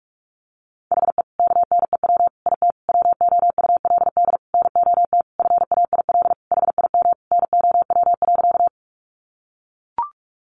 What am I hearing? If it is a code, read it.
Morse code, heard as a keyed tone, about 44 wpm, 700 Hz. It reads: HE KNEW IT WOULD NOT FAIL HIM NOW.
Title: Sentences from Top 1100 Words 45wpm